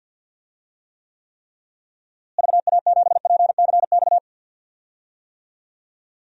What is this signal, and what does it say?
Morse code, heard as a keyed tone, about 52 wpm, 700 Hz. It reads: VA7PCX
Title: DXCC-100 Call Signs - 6 Characters - Repeated 3x - 50wpm